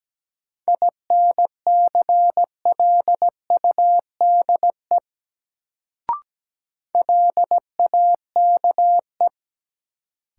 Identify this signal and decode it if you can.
Morse code, heard as a keyed tone, about 17 wpm, 700 Hz. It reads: INCLUDE LAKE
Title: Sentences from Top 1000 Words - Review Part 3 - New Word Drill - 17wpm